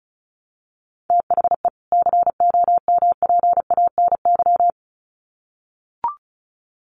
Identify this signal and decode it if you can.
Morse code, heard as a keyed tone, about 34 wpm, 700 Hz. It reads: THE COMPANY
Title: Sets of 2 Words 35wpm